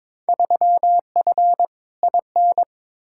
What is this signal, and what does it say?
Morse code, heard as a keyed tone, about 22 wpm, 700 Hz. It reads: 3F IN
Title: Field Day Exchanges - 22wpm